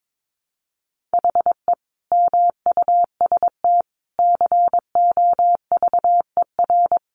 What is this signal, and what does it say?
Morse code, heard as a keyed tone, about 22 wpm, 700 Hz. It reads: HE MUST COVER
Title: Sentences from Top 500 Words 22wpm